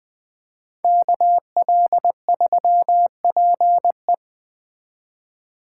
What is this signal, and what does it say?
Morse code, heard as a keyed tone, about 19 wpm, 700 Hz. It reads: KL3PE